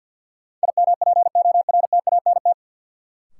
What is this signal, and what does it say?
Morse code, heard as a keyed tone, about 49 wpm, 700 Hz. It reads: IMPORTANT
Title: Sets of 3 Words - New Word Drill - 50wpm